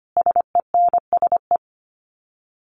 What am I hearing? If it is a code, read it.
Morse code, heard as a keyed tone, about 25 wpm, 700 Hz. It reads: SENSE
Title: Sets of 5 Words Encore2 25wpm